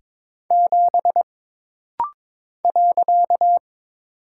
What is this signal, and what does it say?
Morse code, heard as a keyed tone, about 22 wpm, 700 Hz. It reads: 7 .
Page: This rapid-fire timing is geared to push your ICR (Instant Character Recognition) skills.